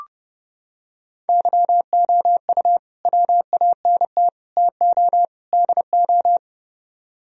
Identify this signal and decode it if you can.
Morse code, heard as a keyed tone, about 30 wpm, 700 Hz. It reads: YOU WANT TO DO